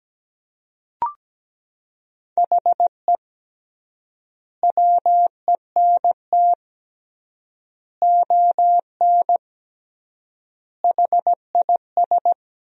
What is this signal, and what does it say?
Morse code, heard as a keyed tone, about 17 wpm, 700 Hz. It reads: HE WENT ON HIS
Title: 3x Word-Spacing - Sentences from Top 200 Words - 17wpm